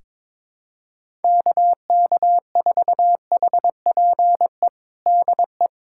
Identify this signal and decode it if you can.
Morse code, heard as a keyed tone, about 22 wpm, 700 Hz. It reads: KK4HPE DE
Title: USA Callsign DE Callsign - 6 Characters - 22wpm